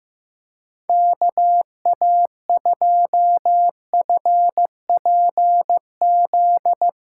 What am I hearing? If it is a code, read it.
Morse code, heard as a keyed tone, about 16 wpm, 700 Hz. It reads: KA2FPZ
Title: USA Callsign DE Callsign - 6 Characters - 15wpm